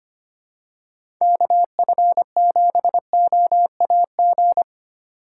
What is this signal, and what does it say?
Morse code, heard as a keyed tone, about 25 wpm, 700 Hz. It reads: KF7OAG
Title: USA Call Signs - 6 Characters - 3x Character Spacing - 25wpm